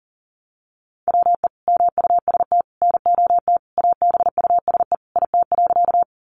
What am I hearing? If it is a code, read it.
Morse code, heard as a keyed tone, about 40 wpm, 700 Hz. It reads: WE MUST NOT ABUSE IT.
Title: Sentences from Top 1300 Words 40wpm